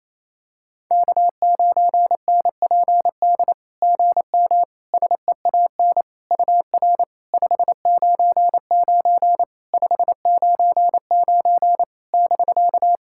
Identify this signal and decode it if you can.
Morse code, heard as a keyed tone, about 28 wpm, 700 Hz. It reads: K9NPD GM SEAN UR 599 599 <BK>
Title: SOTA QSO Practice 28wpm